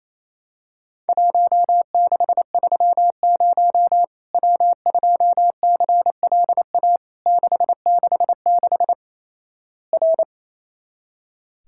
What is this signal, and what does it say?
Morse code, heard as a keyed tone, about 27 wpm, 700 Hz. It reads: 1630 W2CLA 666 R
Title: Worked All Europe DX Contest - QTC Exchange - 28wpm